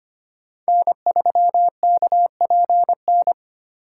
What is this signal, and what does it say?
Morse code, heard as a keyed tone, about 25 wpm, 700 Hz. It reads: N3KPN
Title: USA Call Signs - 5 Characters - 2x Character Spacing - 25wpm